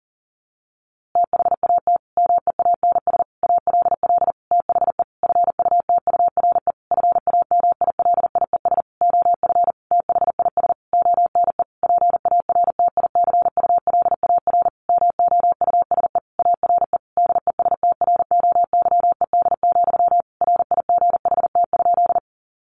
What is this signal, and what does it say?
Morse code, heard as a keyed tone, about 40 wpm, 700 Hz. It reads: THAT MEANS ALL THE FUTURE FAMILIES OF THIS ONE PARTICULAR MOUSE ARE DESTROYED, RIGHT?
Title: A Sound Of Thunder 40wpm